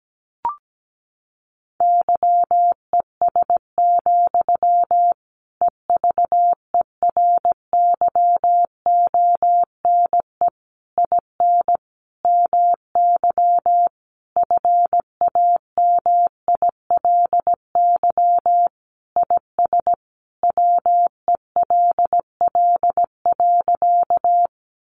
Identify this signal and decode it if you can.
Morse code, heard as a keyed tone, about 17 wpm, 700 Hz. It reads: YES, EVERYONE IN MY FAMILY IS WELL.
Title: Sentences from Top 800 Words 17wpm